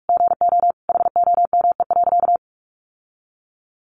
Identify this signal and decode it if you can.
Morse code, heard as a keyed tone, about 44 wpm, 700 Hz. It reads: GO HOME.
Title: Sentences from Top 1400 Words 45wpm